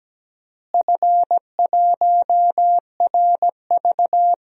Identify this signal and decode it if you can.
Morse code, heard as a keyed tone, about 17 wpm, 700 Hz. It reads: F1RV
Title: DXCC-100 Call Signs - 4 Characters - Repeated 3x - 17wpm